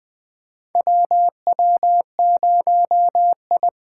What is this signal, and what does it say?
Morse code, heard as a keyed tone, about 20 wpm, 700 Hz. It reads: WW0I
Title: USA Call Signs - 8x Character Spacing - 20wpm